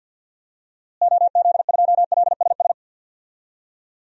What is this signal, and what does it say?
Morse code, heard as a keyed tone, about 50 wpm, 700 Hz. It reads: OZ2LSS
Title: DXCC-100 Call Signs - 6 Characters - Repeated 3x - 50wpm